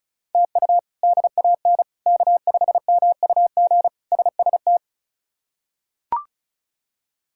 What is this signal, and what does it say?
Morse code, heard as a keyed tone, about 34 wpm, 700 Hz. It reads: TU DAN K5MUG SST
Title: K1USN SST QSO Practice - 35wpm